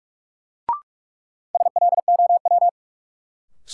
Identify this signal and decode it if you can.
Morse code, heard as a keyed tone, about 46 wpm, 700 Hz. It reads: SLOW
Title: Sentences from Top 400 Words - New Word Drill - 45wpm